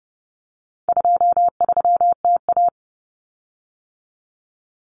Morse code, heard as a keyed tone, about 30 wpm, 700 Hz. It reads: 23TA
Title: Common Cut Numbers - 4-Digit 30wpm